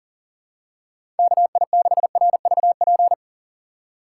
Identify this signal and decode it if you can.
Morse code, heard as a keyed tone, about 42 wpm, 700 Hz. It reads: KI6RVP
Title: USA Call Signs - 6 Characters - 6x Character Spacing - 40wpm